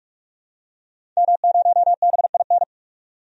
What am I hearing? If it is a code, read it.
Morse code, heard as a keyed tone, about 46 wpm, 700 Hz. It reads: M0BIN
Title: DXCC-100 Call Signs - 5 Characters - 45wpm